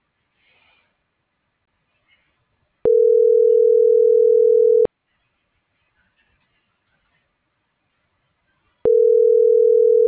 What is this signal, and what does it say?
Ringback tone (North America), 440 + 480 Hz.